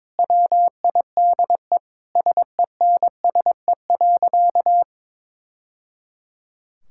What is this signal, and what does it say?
Morse code, heard as a keyed tone, about 22 wpm, 700 Hz. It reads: WIDE SENSE.
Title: Sentences from Top 1000 Words - Review Part 6 - 22wpm